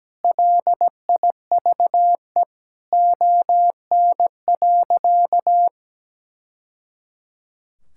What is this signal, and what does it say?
Morse code, heard as a keyed tone, about 17 wpm, 700 Hz. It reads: LIVE ON.
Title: Sentences from Top 1000 Words - Review Part 6 - 17wpm